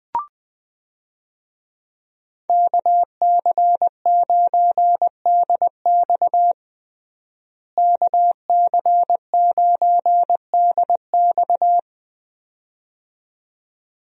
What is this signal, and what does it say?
Morse code, heard as a keyed tone, about 20 wpm, 700 Hz. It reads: KC9DX KC9DX
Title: USA Call Signs - 5 Characters - Repeated 2x - 20wpm